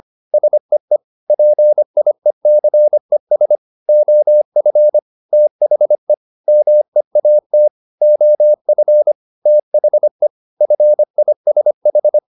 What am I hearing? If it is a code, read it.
Morse code, heard as a keyed tone, about 25 wpm, 600 Hz. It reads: SEE PIECES OF THE MEAT OF THE FISH